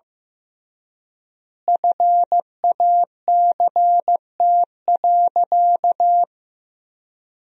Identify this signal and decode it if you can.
Morse code, heard as a keyed tone, about 16 wpm, 700 Hz. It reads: FACT.
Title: 3x Word-Spacing - Sentences from Top 500 Words - 15wpm